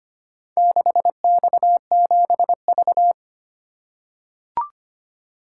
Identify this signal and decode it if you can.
Morse code, heard as a keyed tone, about 25 wpm, 700 Hz. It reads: 6X7V
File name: practice / 25wpm / Four Letter-Number Combinations - 25wpm